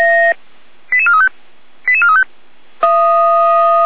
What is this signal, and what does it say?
Five-tone selective call (ZVEI), heard as quick runs of five tones, tones of about 70 ms. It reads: 88435 88435
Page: Index of /455/SDR/ZVEI/88261/2026.03.07